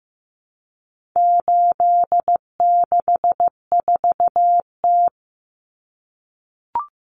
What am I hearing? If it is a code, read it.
Morse code, heard as a keyed tone, about 15 wpm, 700 Hz. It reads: 864T
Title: Common Cut Numbers - 4-Digit 15wpm